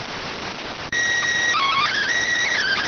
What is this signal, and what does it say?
SSTV (slow-scan television), PD 290.